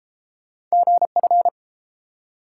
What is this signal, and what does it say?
Morse code, heard as a keyed tone, about 31 wpm, 700 Hz. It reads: GF